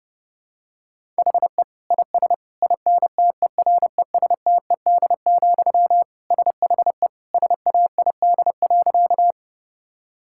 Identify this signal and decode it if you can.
Morse code, heard as a keyed tone, about 30 wpm, 700 Hz. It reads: HE IS INTERESTED, SHE SAID.
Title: Sentences from Top 1000 Words - Review Part 6 - 30wpm